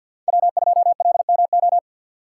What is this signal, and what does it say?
Morse code, heard as a keyed tone, about 52 wpm, 700 Hz. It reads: W2LMO
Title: USA Call Signs - 5 Characters - 4x Character Spacing - 50wpm